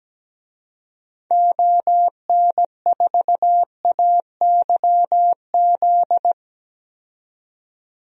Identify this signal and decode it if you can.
Morse code, heard as a keyed tone, about 17 wpm, 700 Hz. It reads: ON4AYZ
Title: DXCC-100 Call Signs - 6 Characters - 17wpm